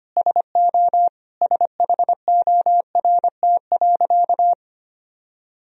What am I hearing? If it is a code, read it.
Morse code, heard as a keyed tone, about 25 wpm, 700 Hz. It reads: SO SHORT.
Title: Sentences from Top 500 Words and Abbreviations 25wpm